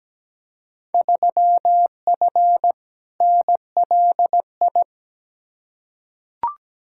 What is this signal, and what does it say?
Morse code, heard as a keyed tone, about 17 wpm, 700 Hz. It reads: 3F NLI
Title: Field Day Exchanges Encore - 17wpm